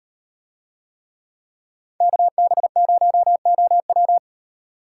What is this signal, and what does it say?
Morse code, heard as a keyed tone, about 36 wpm, 700 Hz. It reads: KB0OW